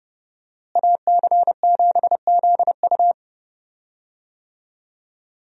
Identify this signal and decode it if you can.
Morse code, heard as a keyed tone, about 31 wpm, 700 Hz. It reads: AC7ZU
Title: USA Call Signs - 5 Characters - Repeated 3x - 30wpm